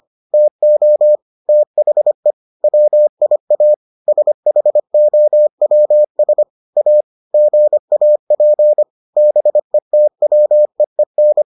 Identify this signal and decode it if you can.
Morse code, heard as a keyed tone, about 25 wpm, 600 Hz. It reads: TO THE WIA SHOWS A GAP BETWEEN